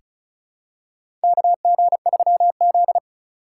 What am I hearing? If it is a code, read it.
Morse code, heard as a keyed tone, about 36 wpm, 700 Hz. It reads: KG3Z
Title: USA Call Signs - 2x Character Spacing - 35wpm